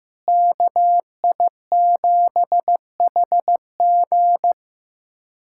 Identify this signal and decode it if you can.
Morse code, heard as a keyed tone, about 15 wpm, 700 Hz. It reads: KI7HG
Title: USA Callsign DE Callsign - 5 Characters - 15wpm